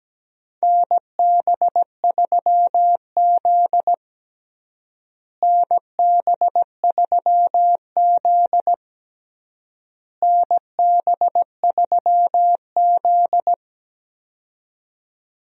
Morse code, heard as a keyed tone, about 17 wpm, 700 Hz. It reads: NB3Z NB3Z NB3Z
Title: USA Call Signs - 4 Characters - Repeated 3x - 17wpm